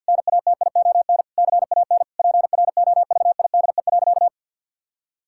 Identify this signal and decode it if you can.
Morse code, heard as a keyed tone, about 51 wpm, 700 Hz. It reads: NATION CAN PROVIDE.
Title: Sentences from Top 1000 Words - Review Part 6 - 50wpm